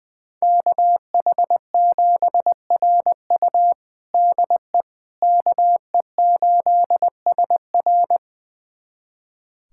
Morse code, heard as a keyed tone, about 20 wpm, 700 Hz. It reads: KH7RU DE KE8SR
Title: USA Callsign DE Callsign - 5 Characters - 20wpm